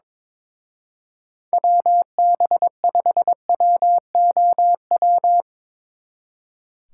Morse code, heard as a keyed tone, about 24 wpm, 700 Hz. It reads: WB5WOW
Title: Speed Racing - USA 2x3 Call Signs 22wpm